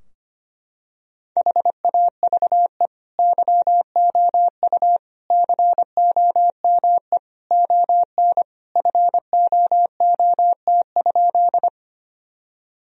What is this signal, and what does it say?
Morse code, heard as a keyed tone, about 25 wpm, 700 Hz. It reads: HAVE YOU COME ON FOOT?
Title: Sentences from Top 1000 Words - Review Part 3 - 25wpm